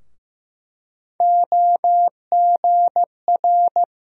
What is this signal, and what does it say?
Morse code, heard as a keyed tone, about 15 wpm, 700 Hz. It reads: OGR
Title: USA Call Sign Postfixes - 3 Letters 15wpm